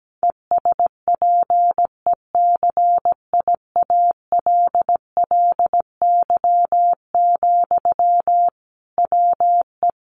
Morse code, heard as a keyed tone, about 17 wpm, 700 Hz. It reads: ESPECIALLY, WE
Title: Sentences from Top 700 Words 17wpm